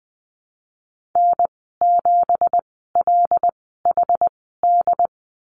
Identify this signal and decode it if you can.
Morse code, heard as a keyed tone, about 20 wpm, spaced as for 15 wpm, 700 Hz. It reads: N7LHD
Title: 193 Lesson 31g - Letter v - Call Signs 20wpm